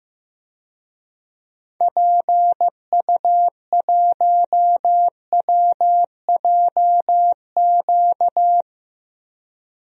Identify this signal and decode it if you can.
Morse code, heard as a keyed tone, about 15 wpm, 700 Hz. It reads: PU1WJQ
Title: DXCC-100 Call Signs - 6 Characters - 15wpm